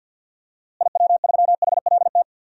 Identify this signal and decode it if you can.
Morse code, heard as a keyed tone, about 46 wpm, 700 Hz. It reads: IW3HLT
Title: DXCC-100 Call Signs - 6 Characters - Repeated 3x - 50wpm